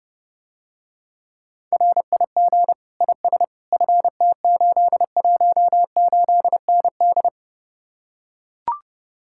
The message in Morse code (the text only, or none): RIG IS FT818ND